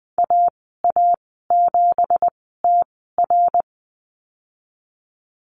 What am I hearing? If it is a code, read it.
Morse code, heard as a keyed tone, about 20 wpm, spaced as for 15 wpm, 700 Hz. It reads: AA7TR
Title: 173 Lesson 28g - Number 9 - Call Signs 20wpm